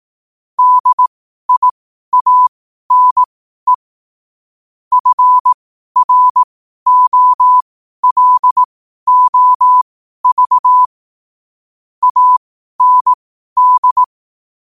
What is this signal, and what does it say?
Morse code, heard as a keyed tone, about 18 wpm, spaced as for 12 wpm, 1000 Hz. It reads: DIANE FROLOV AND
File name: qotdfiles / 12wpm